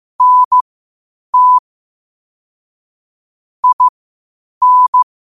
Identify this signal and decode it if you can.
Morse code, heard as a keyed tone, about 15 wpm, spaced as for 8 wpm, 1000 Hz. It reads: NT IN
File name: 8wpm